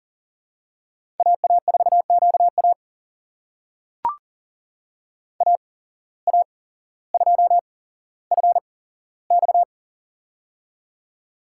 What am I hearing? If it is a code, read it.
Morse code, heard as a keyed tone, about 40 wpm, spaced as for 17 wpm, 700 Hz. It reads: AA4QA AA2FX
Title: USA Call Signs - 5 Characters - 8x Character Spacing - 40wpm